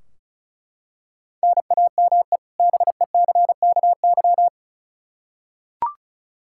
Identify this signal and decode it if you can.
Morse code, heard as a keyed tone, about 36 wpm, 700 Hz. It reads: NAME BECKY
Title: Practice QSO - Exchange 1 - Beginning - 35wpm